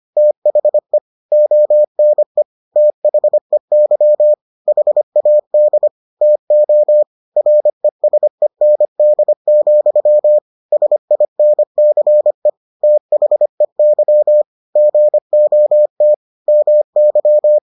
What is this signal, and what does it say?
Morse code, heard as a keyed tone, about 25 wpm, 600 Hz. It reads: THE ONE THEY HAD TO RESEND, SINCE THEY GOT MY